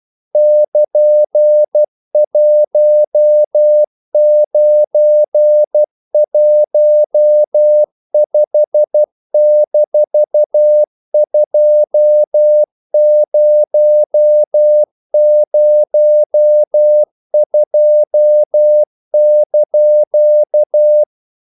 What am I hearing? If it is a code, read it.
Morse code, heard as a keyed tone, about 12 wpm, 600 Hz. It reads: <KN>1915-2002)
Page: English quotations in CW (telegraphy - morse)